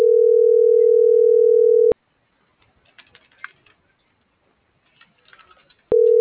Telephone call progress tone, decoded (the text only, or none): ringback tone